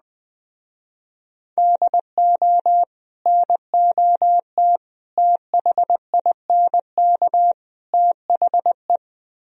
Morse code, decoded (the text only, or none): DO NOT THINK THE